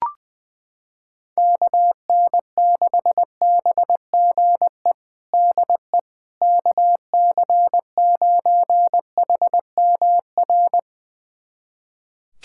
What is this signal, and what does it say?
Morse code, heard as a keyed tone, about 20 wpm, 700 Hz. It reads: KN6BGE DE KC9HMR
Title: USA Callsign DE Callsign 20wpm